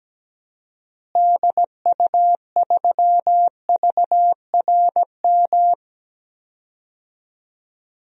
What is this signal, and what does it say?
Morse code, heard as a keyed tone, about 17 wpm, 700 Hz. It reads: DU3VRM
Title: DXCC-100 Call Signs - 6 Characters - Repeated 3x - 17wpm